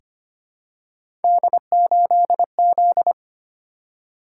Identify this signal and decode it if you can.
Morse code, heard as a keyed tone, about 25 wpm, 700 Hz. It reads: D8Z
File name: Three Letter-Number Combinations - 25wpm